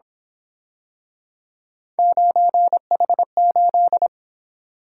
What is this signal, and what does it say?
Morse code, heard as a keyed tone, about 27 wpm, 700 Hz. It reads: 9H8
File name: Speed-Racing - Three Letter-Number Combinations - 22wpm